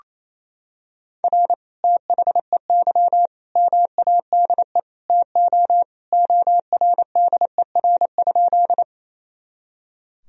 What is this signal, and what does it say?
Morse code, heard as a keyed tone, about 28 wpm, 700 Hz. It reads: R THEY MADE TO ORDER?
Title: Sentences from Top 400 Words and Abbreviations 28wpm